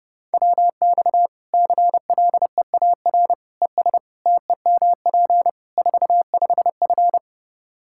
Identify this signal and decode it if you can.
Morse code, heard as a keyed tone, about 29 wpm, 700 Hz. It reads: WX CLEAR ES TEMP 45F
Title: Practice QSO - Exchange 2 - Info - 30wpm